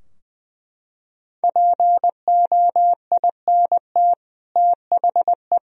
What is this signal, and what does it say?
Morse code, heard as a keyed tone, about 20 wpm, 700 Hz. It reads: POINT THE
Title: Sentences from Top 1000 Words - Review Part 2 - 20wpm